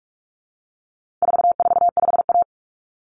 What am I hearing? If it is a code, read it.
Morse code, heard as a keyed tone, about 44 wpm, 700 Hz. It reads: 445A
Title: Common Cut Numbers - 4-Digit 45wpm